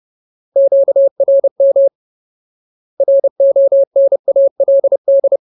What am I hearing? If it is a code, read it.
Morse code, heard as a keyed tone, about 30 wpm, 550 Hz. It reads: QRM RONALD
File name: Ultimate Word List / 30WPM Ultimate